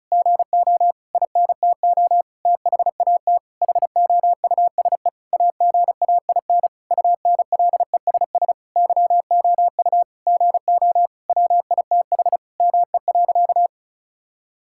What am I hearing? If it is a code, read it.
Morse code, heard as a keyed tone, about 35 wpm, 700 Hz. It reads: GO INTO THAT HOUSE AGAIN UNLESS YOU GO WITH ME.
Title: Sentences from Top 1000 Words - Review Part 6 - 35wpm